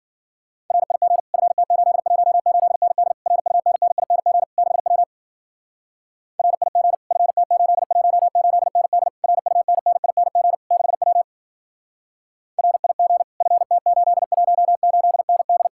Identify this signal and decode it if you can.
Morse code, heard as a keyed tone, about 49 wpm, 700 Hz. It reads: RIG FT818ND RUNNING 6W RIG FT818ND RUNNING 6W RIG FT818ND
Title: Speed-Racing Common QSO Phrases - 40wpm